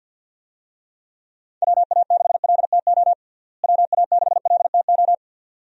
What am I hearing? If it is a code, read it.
Morse code, heard as a keyed tone, about 46 wpm, 700 Hz. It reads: WA6LTO WA6LTO
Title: USA Call Signs - 6 Characters - Repeated 2x - 50wpm